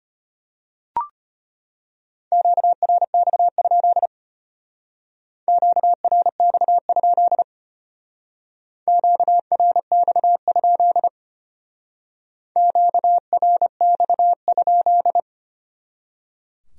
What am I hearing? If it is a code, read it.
Morse code, heard as a keyed tone, about 33 wpm, 700 Hz. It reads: QRX? QRX? QRX? QRX?